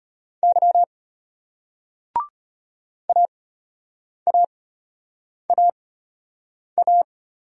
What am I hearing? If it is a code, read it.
Morse code, heard as a keyed tone, about 34 wpm, spaced as for 8 wpm, 700 Hz. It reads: Y AAAA